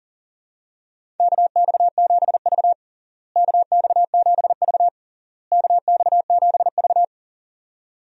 Morse code, heard as a keyed tone, about 40 wpm, 700 Hz. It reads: KX7V KX7V KX7V
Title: USA Call Signs - 4 Characters - Repeated 3x - 40wpm